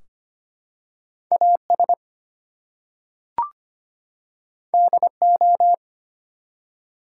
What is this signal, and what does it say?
Morse code, heard as a keyed tone, about 25 wpm, 700 Hz. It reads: AS DO
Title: Two Letter Words - 25wpm